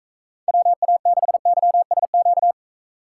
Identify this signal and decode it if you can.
Morse code, heard as a keyed tone, about 44 wpm, 700 Hz. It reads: WA6YSQ